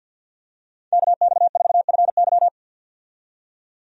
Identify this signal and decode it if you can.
Morse code, heard as a keyed tone, about 53 wpm, 700 Hz. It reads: KX4FY
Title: USA Call Signs - 5 Characters - 50wpm